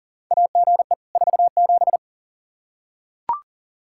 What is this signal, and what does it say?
Morse code, heard as a keyed tone, about 40 wpm, 700 Hz. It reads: AGE 47
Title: K1USN SST QSO Practice 40wpm